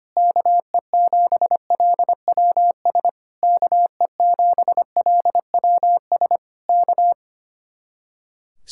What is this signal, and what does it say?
Morse code, heard as a keyed tone, about 24 wpm, 700 Hz. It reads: KE7LWS KE7LWS K
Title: Speed-Racing Common QSO Phrases - 25wpm